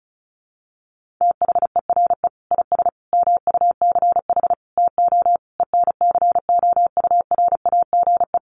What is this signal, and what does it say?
Morse code, heard as a keyed tone, about 34 wpm, 700 Hz. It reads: THERE IS MUCH TO ENCOURAGE
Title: Sentences from Top 1400 Words 35wpm